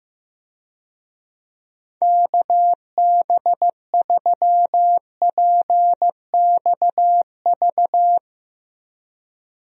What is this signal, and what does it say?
Morse code, heard as a keyed tone, about 15 wpm, 700 Hz. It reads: KB3PXV
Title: USA Call Signs - 6 Characters - Repeated 3x - 15wpm